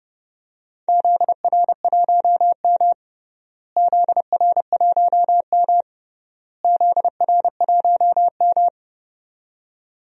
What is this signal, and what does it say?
Morse code, heard as a keyed tone, about 30 wpm, 700 Hz. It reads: ZR1M ZR1M ZR1M
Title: DXCC-100 Call Signs - 4 Characters - Repeated 3x - 30wpm